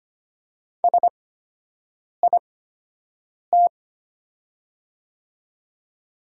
Morse code, heard as a keyed tone, about 25 wpm, spaced as for 7 wpm, 690 Hz. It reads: SIT
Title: 8x Character Spacing - Top 500 Words - 25wpm